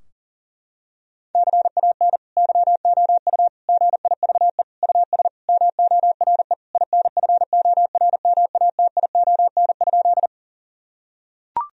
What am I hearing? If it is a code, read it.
Morse code, heard as a keyed tone, about 40 wpm, 700 Hz. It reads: CAN YOU GIVE US MORE INFORMATION?
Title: Sentences from Top 1000 Words - Review Part 3 - 40wpm